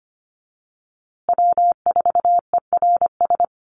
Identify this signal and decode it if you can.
Morse code, heard as a keyed tone, about 25 wpm, 700 Hz. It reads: W4ERS
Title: USA Call Signs Repeated 2x 25wpm